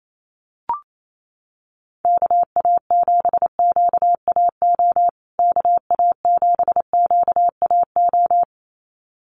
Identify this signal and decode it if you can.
Morse code, heard as a keyed tone, about 28 wpm, 700 Hz. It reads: KA7QAO KA7QAO
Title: USA Call Signs Repeated 2x 28wpm